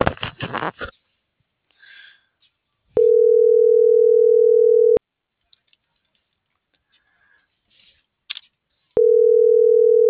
Ringback tone (North America), 440 + 480 Hz.